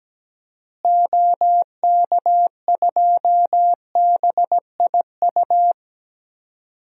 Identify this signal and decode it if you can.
Morse code, heard as a keyed tone, about 17 wpm, 700 Hz. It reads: OK2BIU
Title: DXCC-100 Call Signs - 6 Characters - Repeated 2x - 17wpm